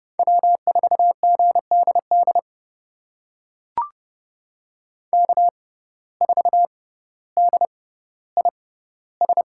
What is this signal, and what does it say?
Morse code, heard as a keyed tone, about 32 wpm, spaced as for 16 wpm, 700 Hz. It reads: W4GDD K4DIS
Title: USA Call Signs - 5 Characters - 6x Character Spacing - 30wpm